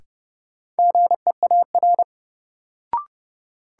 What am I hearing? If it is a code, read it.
Morse code, heard as a keyed tone, about 30 wpm, 700 Hz. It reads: GEAR
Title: Sentences from Top 1400 Words - New Word Drill Alt - 30wpm